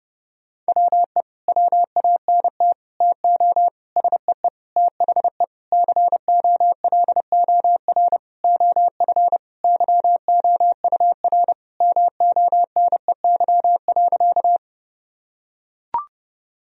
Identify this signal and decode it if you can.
Morse code, heard as a keyed tone, about 31 wpm, 700 Hz. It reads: WE WANT TO SEE THE COLOR OF YOUR MONEY.
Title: Sentences from Top 1000 Words - Review Part 2 - 30wpm